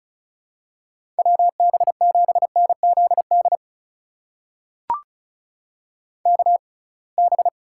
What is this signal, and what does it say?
Morse code, heard as a keyed tone, about 34 wpm, spaced as for 26 wpm, 700 Hz. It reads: WB7NZD KB
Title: USA Call Signs - 6 Characters - 6x Character Spacing - 35wpm